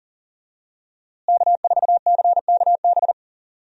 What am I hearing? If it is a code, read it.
Morse code, heard as a keyed tone, about 40 wpm, 700 Hz. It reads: K4CKB
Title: USA Call Signs - 5 Characters - Repeated 3x - 40wpm